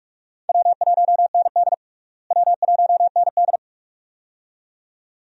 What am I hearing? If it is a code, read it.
Morse code, heard as a keyed tone, about 46 wpm, 700 Hz. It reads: W1ND W1ND
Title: USA Call Signs - 4 Characters - Repeated 2x - 45wpm